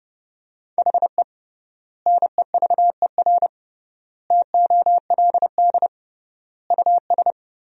Morse code, heard as a keyed tone, about 29 wpm, 700 Hz. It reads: HE NEVER TOLD US
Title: 3x Word-Spacing - Sentences from Top 400 Words - 30wpm